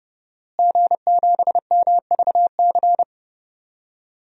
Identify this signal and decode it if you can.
Morse code, heard as a keyed tone, about 30 wpm, 700 Hz. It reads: G7MVC
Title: DXCC-100 Call Signs - 5 Characters - Repeated 3x - 30wpm